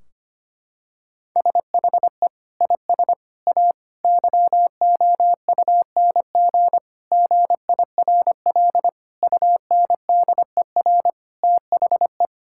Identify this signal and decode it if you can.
Morse code, heard as a keyed tone, about 25 wpm, 700 Hz. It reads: SHE IS A YOUNG GIRL UNDER THE
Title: Sentences from Top 1600 Words 25wpm